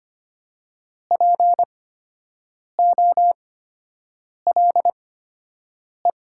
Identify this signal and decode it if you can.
Morse code, heard as a keyed tone, about 25 wpm, spaced as for 7 wpm, 700 Hz. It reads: POLE
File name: Top 90 QSO Elements - 8x Character Spacing - 25wpm